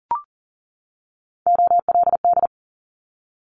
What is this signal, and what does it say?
Morse code, heard as a keyed tone, about 38 wpm, 690 Hz. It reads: OLD
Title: Multi-letter Phonemes 40wpm